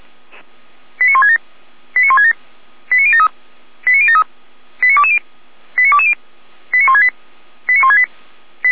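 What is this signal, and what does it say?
Five-tone selective call (ZVEI), heard as quick runs of five tones, tones of about 70 ms. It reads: 89167 89167 89073 89073 89220 89220 89168 89168
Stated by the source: Five-tone ZVEI selective call received on an SDR, channel 455, archived by Feuerwehr Orlamünde.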